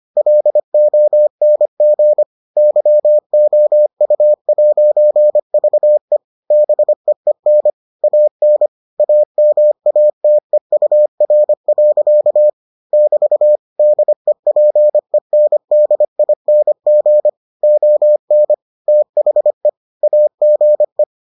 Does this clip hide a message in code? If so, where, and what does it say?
Yes: Morse code from 0.2 to 21.0 s, heard as a keyed tone, about 25 wpm, 600 Hz. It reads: LONG YOU'VE BEEN AN AMATEUR. <BT> DEPENDING ON THE AGE